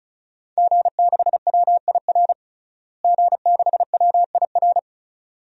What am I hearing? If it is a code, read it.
Morse code, heard as a keyed tone, about 36 wpm, 700 Hz. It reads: G6WIR G6WIR
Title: DXCC-100 Call Signs - 5 Characters - Repeated 2x - 35wpm